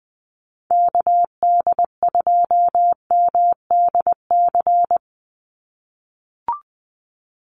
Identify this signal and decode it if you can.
Morse code, heard as a keyed tone, about 20 wpm, 700 Hz. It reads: KD2MDC
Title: USA Call Signs 20wpm